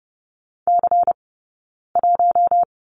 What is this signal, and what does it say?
Morse code, heard as a keyed tone, about 30 wpm, 700 Hz. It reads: C 1
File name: Lesson 20d - Number 3 - Two Characters 30wpm